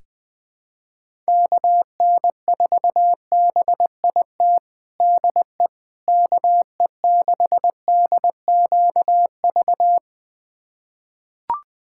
Morse code, heard as a keyed tone, about 21 wpm, 700 Hz. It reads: KN4BIT DE KE6DQV
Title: USA Callsign DE Callsign - 6 Characters - 20wpm